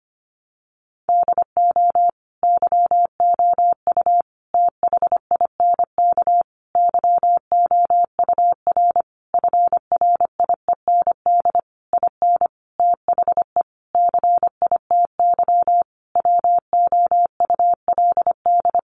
Morse code, heard as a keyed tone, about 25 wpm, 700 Hz. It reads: DO YOU THINK YOUR FRIEND IN THE CITY WOULD
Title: Sentences from Top 1100 Words 25wpm